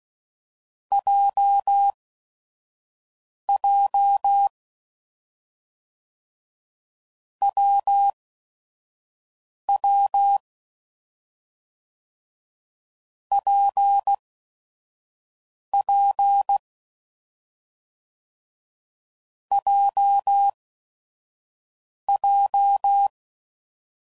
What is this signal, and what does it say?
Morse code, heard as a keyed tone, about 16 wpm, spaced as for 5 wpm, 800 Hz. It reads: JJ WW PP JJ
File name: Combinations / Difficult Combos / AWJ1